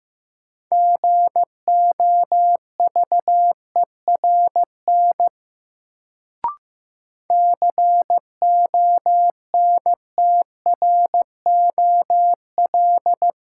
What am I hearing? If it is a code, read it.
Morse code, heard as a keyed tone, about 15 wpm, 700 Hz. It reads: GOVERN CONTROL
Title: Sentences from Top 1000 Words - Review Part 2 - New Word Drill - 15wpm